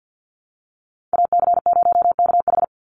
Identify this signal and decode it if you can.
Morse code, heard as a keyed tone, about 49 wpm, 700 Hz. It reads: AC0KH
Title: USA Call Signs 50wpm